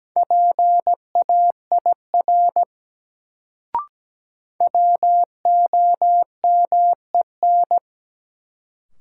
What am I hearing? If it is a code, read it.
Morse code, heard as a keyed tone, about 17 wpm, 700 Hz. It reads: PAIR WOMEN
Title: Sets of 3 Words - New Word Drill - 17wpm